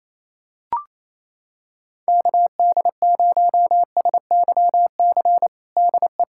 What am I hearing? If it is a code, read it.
Morse code, heard as a keyed tone, about 28 wpm, 700 Hz. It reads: KD0SYC DE
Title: USA Callsign DE Callsign 28wpm